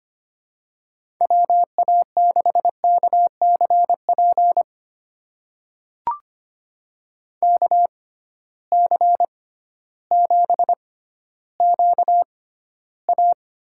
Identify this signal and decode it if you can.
Morse code, heard as a keyed tone, about 25 wpm, spaced as for 13 wpm, 700 Hz. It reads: WA6KCP KC7QA
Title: USA Call Signs - 6x Character Spacing - 25wpm